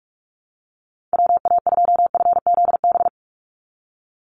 Morse code, heard as a keyed tone, about 44 wpm, 700 Hz. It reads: WA2FZB
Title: Call Signs Repeated 2x - 45wpm